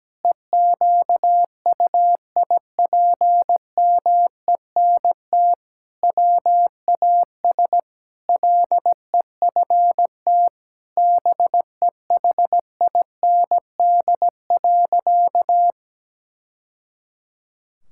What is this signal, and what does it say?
Morse code, heard as a keyed tone, about 17 wpm, 700 Hz. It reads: EQUIPMENT WAS LEFT BEHIND.
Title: Sentences from Top 1000 Words - Review Part 5 - 17wpm